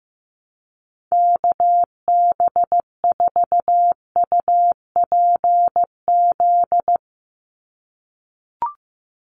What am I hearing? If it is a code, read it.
Morse code, heard as a keyed tone, about 15 wpm, 700 Hz. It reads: KB4UPZ
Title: USA Call Signs 15wpm